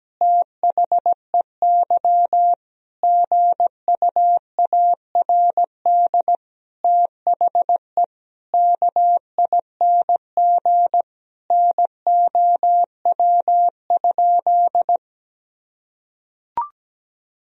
Morse code, heard as a keyed tone, about 17 wpm, 700 Hz. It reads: THEY GUARD THE KING NOW?
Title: Sentences from Top 1000 Words - Review Part 4 - 17wpm